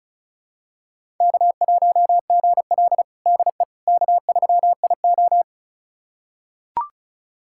Morse code, heard as a keyed tone, about 36 wpm, 700 Hz. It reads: K1GL DE K3IO
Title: USA Callsign DE Callsign - 4 Characters - 35wpm